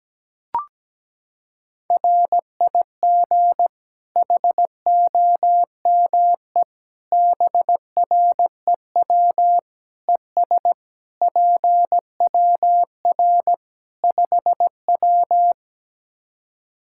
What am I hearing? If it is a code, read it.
Morse code, heard as a keyed tone, about 17 wpm, 700 Hz. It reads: RIG HOME BREW ES PWR 5W
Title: K1USN SST QSO Practice 17wpm